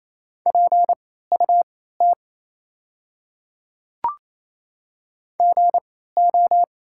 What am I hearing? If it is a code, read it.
Morse code, heard as a keyed tone, about 28 wpm, spaced as for 16 wpm, 700 Hz. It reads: PUT GO
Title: 3x Character Spacing - Top 200 Words - 28wpm